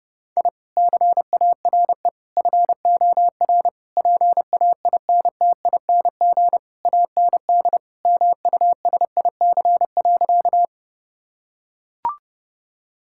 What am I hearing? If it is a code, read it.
Morse code, heard as a keyed tone, about 30 wpm, 700 Hz. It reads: I CARE FOR PAINTING AND MUSIC.
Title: Sentences from Top 1000 Words - Review Part 6 - 30wpm